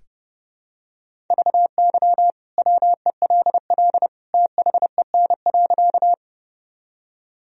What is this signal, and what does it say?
Morse code, heard as a keyed tone, about 30 wpm, 700 Hz. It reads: VY WELL THEN.
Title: Sentences from Top 200 Words and Abbreviations 30wpm